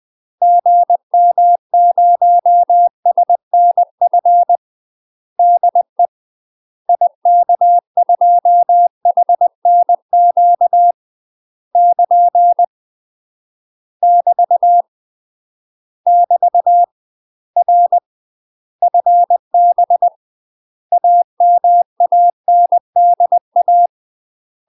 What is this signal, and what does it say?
Morse code, heard as a keyed tone, about 20 wpm, 700 Hz. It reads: GM0SNF DE IK2HNQ <KN> <BT> <BT> R FB AMANDA
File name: UK MASTER QSOs TRAINING / 20WPM / UK MASTER QSOs 0060